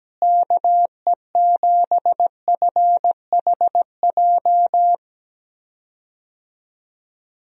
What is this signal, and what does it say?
Morse code, heard as a keyed tone, about 17 wpm, 700 Hz. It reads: KE7FHJ
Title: USA Call Signs - 6 Characters - Repeated 2x - 17wpm